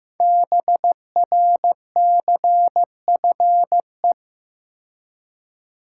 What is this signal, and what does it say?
Morse code, heard as a keyed tone, about 15 wpm, 700 Hz. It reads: BRCFE
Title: Code Groups - 5 Letters - 15wpm